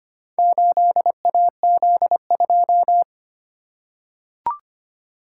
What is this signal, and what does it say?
Morse code, heard as a keyed tone, about 25 wpm, 700 Hz. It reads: 8AZ2
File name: practice / 25wpm / Four Letter-Number Combinations - 25wpm